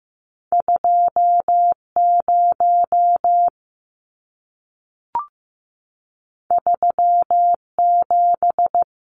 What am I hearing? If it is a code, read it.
Morse code, heard as a keyed tone, about 15 wpm, 700 Hz. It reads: 20 37
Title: Numbers - 2-Digit 15wpm